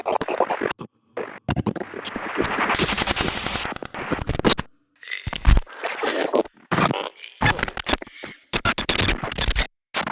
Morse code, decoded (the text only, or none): none